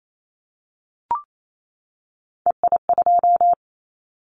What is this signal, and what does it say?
Morse code, heard as a keyed tone, about 28 wpm, 700 Hz. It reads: EI2
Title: Callsign Prefixes for Top 50 DXCC - 28wpm